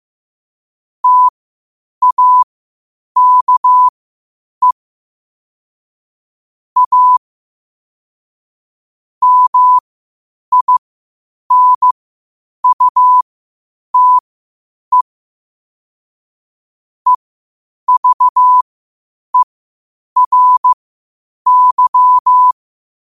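Morse code, heard as a keyed tone, about 15 wpm, spaced as for 8 wpm, 1000 Hz. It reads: TAKE A MINUTE EVERY
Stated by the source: Morse code quotes at 8 WPM.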